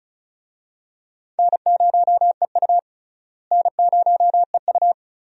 Morse code, heard as a keyed tone, about 36 wpm, 700 Hz. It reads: N0EU N0EU
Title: USA Call Signs - 4 Characters - Repeated 3x - 35wpm